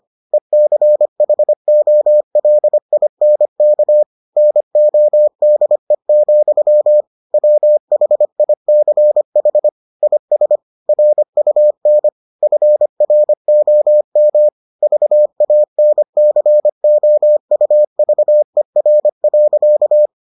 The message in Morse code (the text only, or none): ECHOLINK NODE, WHICH IS RUN FROM VANCOUVER.